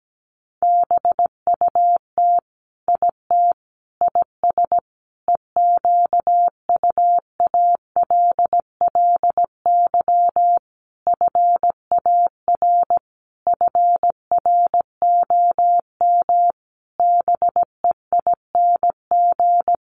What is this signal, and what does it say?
Morse code, heard as a keyed tone, about 17 wpm, 700 Hz. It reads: BUT IT IS EQUALLY FAR FROM BEING
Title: Sentences from Top 1400 Words 17wpm